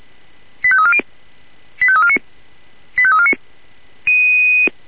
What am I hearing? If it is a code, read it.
Five-tone selective call (ZVEI), heard as quick runs of five tones, tones of about 70 ms. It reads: 85340 85349 85349